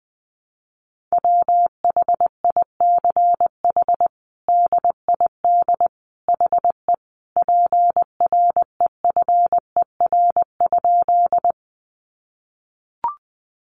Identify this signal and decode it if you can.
Morse code, heard as a keyed tone, about 19 wpm, 700 Hz. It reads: WHICH DID HE PREFER?
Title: Sentences from Top 1100 Words 20wpm